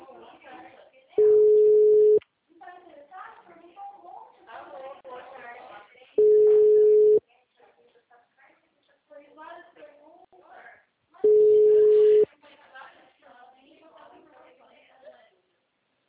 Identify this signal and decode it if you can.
Ringback tone (Europe, 425 Hz), with speech.